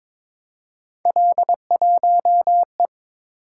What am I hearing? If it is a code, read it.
Morse code, heard as a keyed tone, about 22 wpm, 700 Hz. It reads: L1E
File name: practice / 22wpm / Three Letter-Number Combinations - 22wpm